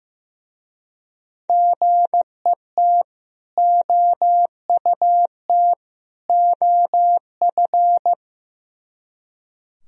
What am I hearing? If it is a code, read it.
Morse code, heard as a keyed tone, about 16 wpm, 700 Hz. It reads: GET OUT OF
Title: Speed-Racing - Sets of 3 Words 15wpm